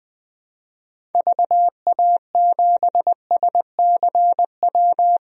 Morse code, heard as a keyed tone, about 20 wpm, 700 Hz. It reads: VA7SCW
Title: DXCC-100 Call Signs - 6 Characters - 20wpm